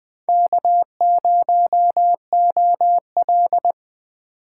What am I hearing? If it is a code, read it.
Morse code, heard as a keyed tone, about 23 wpm, 700 Hz. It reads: K0OL
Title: Speed Racing - USA 1x2 Call Signs 20wpm